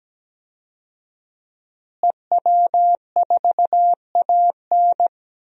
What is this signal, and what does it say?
Morse code, heard as a keyed tone, about 17 wpm, 700 Hz. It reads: EW4AN
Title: DXCC-100 Call Signs - 5 Characters - Repeated 3x - 17wpm